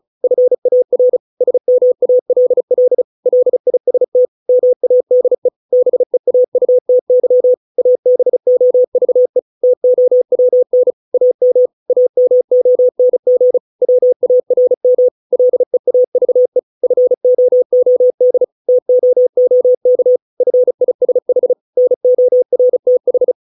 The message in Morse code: FAR SMALL LIST MADE BEAUTY ABOVE TOWN AM AMONG WARM LEAVE FOOD TOOK FISH NORTH

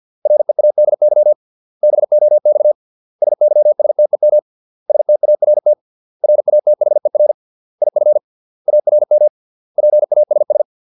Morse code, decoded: READY BOX SYSTEM START RATHER IF ARM PASS

